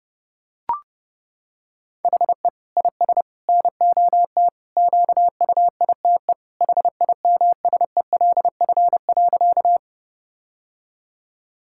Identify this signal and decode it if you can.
Morse code, heard as a keyed tone, about 30 wpm, 700 Hz. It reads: HE IS NOT QUITE HIMSELF.
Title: Sentences from Top 1600 Words 30wpm